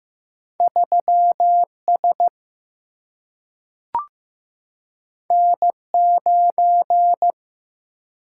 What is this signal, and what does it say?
Morse code, heard as a keyed tone, about 15 wpm, 700 Hz. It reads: 3S N9
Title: Single-Letter Single-Number 15wpm